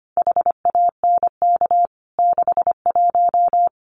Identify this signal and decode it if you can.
Morse code, heard as a keyed tone, about 25 wpm, 700 Hz. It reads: HANK 61
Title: CWT Exchange 25wpm